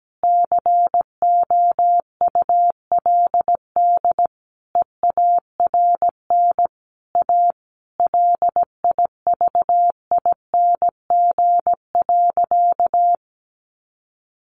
Morse code, heard as a keyed tone, about 17 wpm, 700 Hz. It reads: COULD EARN A LIVING.
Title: Sentences from Top 1500 Words 17wpm